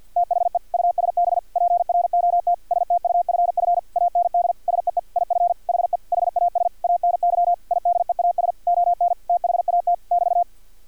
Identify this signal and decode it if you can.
Morse code, heard as a keyed tone, about 50 wpm, 700 Hz. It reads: THE USB PROT STUFF AND SEE IF HE HAS ANY IDEAS ON THAT <BT>
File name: archived / CRSnet